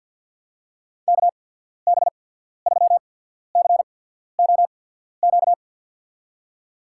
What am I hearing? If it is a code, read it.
Morse code, heard as a keyed tone, about 53 wpm, 700 Hz. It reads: K B 3 C O Q
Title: USA Call Signs - 6 Characters - 8x Character Spacing - 50wpm